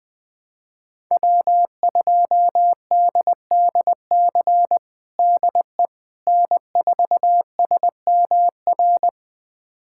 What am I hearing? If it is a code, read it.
Morse code, heard as a keyed tone, about 20 wpm, 700 Hz. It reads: W2DDC DE N4SMR
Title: USA Callsign DE Callsign - 5 Characters - 20wpm